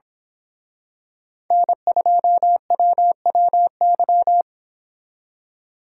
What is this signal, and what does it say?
Morse code, heard as a keyed tone, about 26 wpm, 700 Hz. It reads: N2WWY
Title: Speed Racing - USA 1x3 Call Signs 22wpm